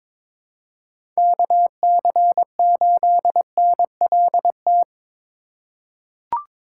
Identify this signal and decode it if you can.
Morse code, heard as a keyed tone, about 22 wpm, 700 Hz. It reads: KC8NLT
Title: USA Call Signs - 6 Characters - 6x Character Spacing - 22wpm